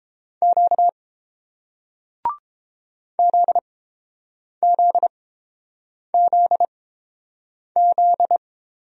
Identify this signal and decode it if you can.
Morse code, heard as a keyed tone, about 30 wpm, spaced as for 7 wpm, 700 Hz. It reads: Q ZZZZ